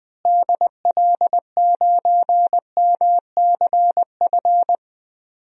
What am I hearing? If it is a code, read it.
Morse code, heard as a keyed tone, about 20 wpm, 700 Hz. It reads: DL9MCF
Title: DXCC-100 Call Signs - 6 Characters - Repeated 3x - 20wpm